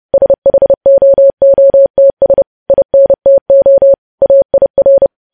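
Morse code, heard as a keyed tone, about 30 wpm, 550 Hz. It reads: SHOOTS INTO AIR
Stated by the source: Morse news headlines sent at 30.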